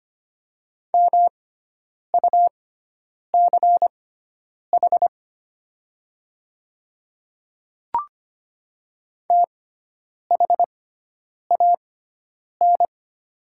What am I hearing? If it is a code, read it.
Morse code, heard as a keyed tone, about 25 wpm, spaced as for 9 wpm, 690 Hz. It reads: MUCH THAN
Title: 6x Character Spacing - Top 200 Words - 25wpm